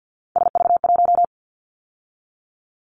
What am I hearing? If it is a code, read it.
Morse code, heard as a keyed tone, about 50 wpm, 700 Hz. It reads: SV1
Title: Callsign Prefixes for Top 50 DXCC - 50wpm